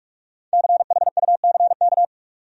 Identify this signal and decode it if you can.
Morse code, heard as a keyed tone, about 46 wpm, 700 Hz. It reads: CHUCK
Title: USA Common Names - 45wpm